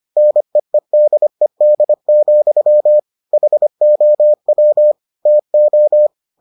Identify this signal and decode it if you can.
Morse code, heard as a keyed tone, about 25 wpm, 600 Hz. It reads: NEEDED, HOW TO